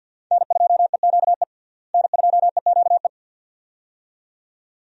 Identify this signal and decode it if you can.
Morse code, heard as a keyed tone, about 51 wpm, 700 Hz. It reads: N2EQE N2EQE
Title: USA Call Signs - 5 Characters - Repeated 2x - 50wpm